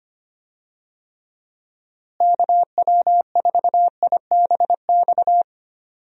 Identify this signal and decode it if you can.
Morse code, heard as a keyed tone, about 25 wpm, 700 Hz. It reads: KW4IBX
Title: USA Call Signs - 6 Characters - Repeated 3x - 25wpm